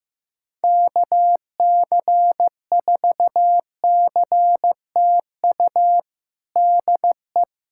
Morse code, heard as a keyed tone, about 15 wpm, 700 Hz. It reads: KC4CTU DE
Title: USA Callsign DE Callsign - 6 Characters - 15wpm